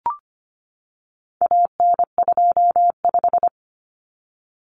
Morse code, heard as a keyed tone, about 25 wpm, 700 Hz. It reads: AN25
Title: Common Cut Numbers - 4-Digit 25wpm